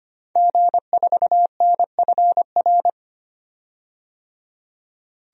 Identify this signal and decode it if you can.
Morse code, heard as a keyed tone, about 25 wpm, 700 Hz. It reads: G4NFR
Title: DXCC-100 Call Signs - 5 Characters - 25wpm